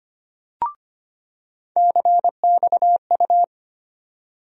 Morse code, heard as a keyed tone, about 25 wpm, 700 Hz. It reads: CXU
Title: USA Call Sign Postfixes - 3 Letters 25wpm